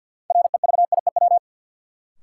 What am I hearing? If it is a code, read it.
Morse code, heard as a keyed tone, about 53 wpm, 700 Hz. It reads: REVIEW
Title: Sentences from Top 900 Words - New Word Drill - 50wpm